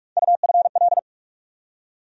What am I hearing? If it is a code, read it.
Morse code, heard as a keyed tone, about 45 wpm, 700 Hz. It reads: UFL
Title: USA Call Sign Postfixes - 3 Letters 45wpm